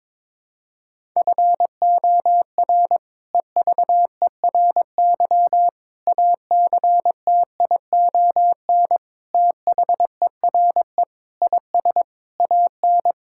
Morse code, heard as a keyed tone, about 21 wpm, 700 Hz. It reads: FOR EVERY ACTION THERE IS AN
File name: practice / 22wpm / Sentences with 8-Word Phrases - 22wpm